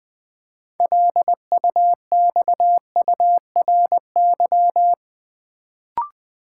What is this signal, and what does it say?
Morse code, heard as a keyed tone, about 20 wpm, 700 Hz. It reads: LUXURY
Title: Build A Word - 2 Letters At A Time - 6-Letter Words - 20wpm